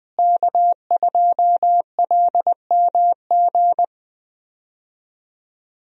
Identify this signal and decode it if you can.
Morse code, heard as a keyed tone, about 20 wpm, 700 Hz. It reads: K2LMG
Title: USA Call Signs - 5 Characters - Repeated 3x - 20wpm